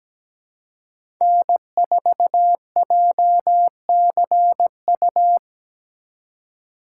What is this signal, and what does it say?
Morse code, heard as a keyed tone, about 17 wpm, 700 Hz. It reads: N4JCU
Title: USA Call Signs - 5 Characters - 2x Character Spacing - 17wpm